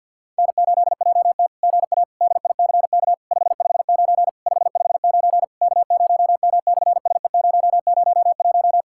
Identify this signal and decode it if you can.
Morse code, heard as a keyed tone, about 50 wpm, 700 Hz. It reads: N8JT GA DICK 559 559 K0M/SE001
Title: SOTA QSO Practice 50wpm